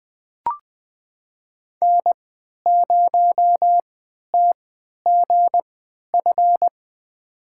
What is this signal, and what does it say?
Morse code, heard as a keyed tone, about 21 wpm, spaced as for 11 wpm, 700 Hz. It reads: N0TGF
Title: USA Call Signs - 5 Characters - 3x Character Spacing - 20wpm